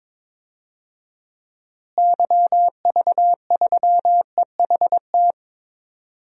Morse code, heard as a keyed tone, about 21 wpm, 700 Hz. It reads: YV3EHT
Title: DXCC-100 Call Signs - 6 Characters - Repeated 3x - 22wpm